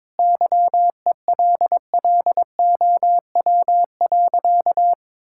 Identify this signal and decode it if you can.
Morse code, heard as a keyed tone, about 22 wpm, 700 Hz. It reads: YELLOW.
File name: practice / 22wpm / Easy Sentences with Hints - 22wpm